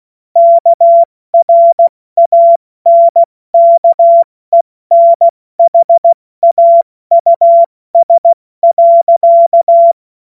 Morse code, heard as a keyed tone, about 16 wpm, 680 Hz. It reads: KRANKENHAUS.
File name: Morsekurs / Witze / Witz 03. August 2021, 70 F80